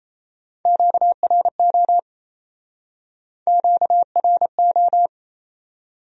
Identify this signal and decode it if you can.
Morse code, heard as a keyed tone, about 39 wpm, 700 Hz. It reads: QRO QRO